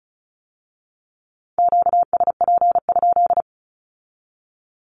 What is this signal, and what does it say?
Morse code, heard as a keyed tone, about 36 wpm, 700 Hz. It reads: QSP?
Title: Q-Codes with Spoken Meaning - 35wpm